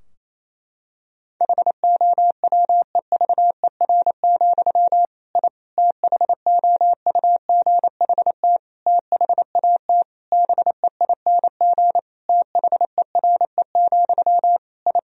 Morse code, heard as a keyed tone, about 27 wpm, 700 Hz. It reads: HOWEVER, I THOUGHT THAT BEING THERE, I
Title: Sentences from Top 1000 Words - Review Part 3 - 28wpm